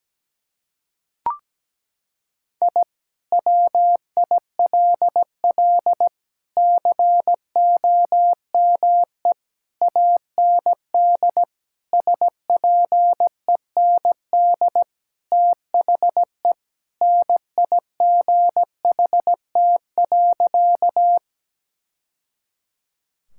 Morse code, heard as a keyed tone, about 17 wpm, 700 Hz. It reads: I WILL COME AND SPEND THE NIGHT.
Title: Sentences from Top 1000 Words - Review Part 5 - 17wpm